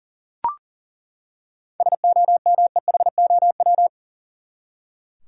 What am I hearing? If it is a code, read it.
Morse code, heard as a keyed tone, about 40 wpm, 700 Hz. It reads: SOMEHOW
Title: Seven Letter Words - 40wpm